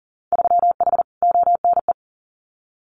Morse code, heard as a keyed tone, about 41 wpm, 700 Hz. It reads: 3H ONE
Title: Winter Field Day Exchanges Encore 40wpm